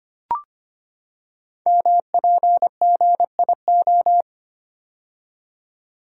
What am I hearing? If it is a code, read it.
Morse code, heard as a keyed tone, about 25 wpm, 700 Hz. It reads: MPGIO
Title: Code Groups - 5 Letters - 25wpm